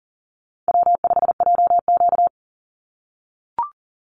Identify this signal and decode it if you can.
Morse code, heard as a keyed tone, about 38 wpm, 700 Hz. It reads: W5JQ
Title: Easy Call Signs - 40wpm